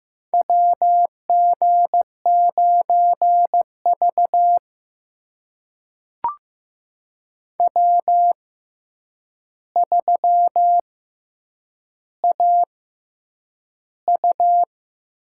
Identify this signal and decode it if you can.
Morse code, heard as a keyed tone, about 15 wpm, spaced as for 8 wpm, 700 Hz. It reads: WG9V W3AU
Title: USA Call Signs - 4 Characters - 6x Character Spacing - 15wpm